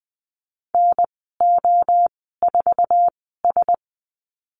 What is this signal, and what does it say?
Morse code, heard as a keyed tone, about 20 wpm, spaced as for 14 wpm, 700 Hz. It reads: NO4S
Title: 062 Lesson 11g - Letter h - Call Signs 20wpm